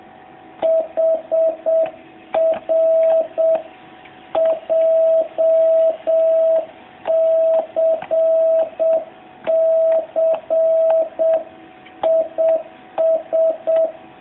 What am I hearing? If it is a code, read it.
Morse code, heard as a keyed tone, about 7 wpm, 650 Hz. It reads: HR JCC IS
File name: jh8cbh / 2019cw